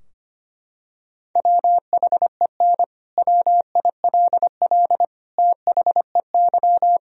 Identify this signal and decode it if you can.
Morse code, heard as a keyed tone, about 25 wpm, 700 Hz. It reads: WHEN WILL THEY
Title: Sentences from Top 1000 Words - Review Part 3 - 25wpm